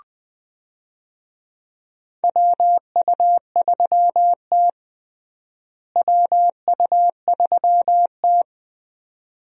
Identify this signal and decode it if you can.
Morse code, heard as a keyed tone, about 20 wpm, 700 Hz. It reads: WU3T WU3T
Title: USA Call Signs - 4 Characters - Repeated 2x - 20wpm